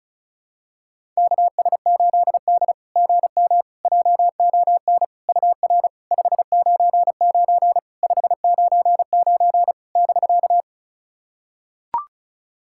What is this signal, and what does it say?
Morse code, heard as a keyed tone, about 36 wpm, 690 Hz. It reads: KS8D GM JON UR 599 599 <BK>
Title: SOTA QSO Practice 35wpm